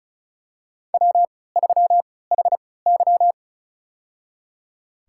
Morse code, heard as a keyed tone, about 34 wpm, spaced as for 20 wpm, 700 Hz. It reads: W3HY
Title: USA Call Signs - 3x Character Spacing - 35wpm